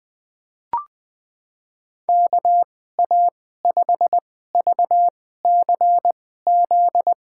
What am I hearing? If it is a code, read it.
Morse code, heard as a keyed tone, about 20 wpm, spaced as for 14 wpm, 700 Hz. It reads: KA5VCZ
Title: USA Call Signs - 6 Characters - 2x Character Spacing - 20wpm